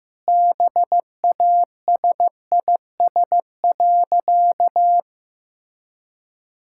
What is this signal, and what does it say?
Morse code, heard as a keyed tone, about 15 wpm, 700 Hz. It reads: BASIS.
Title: Sentences from Top 1000 Words - Review Part 5 - 15wpm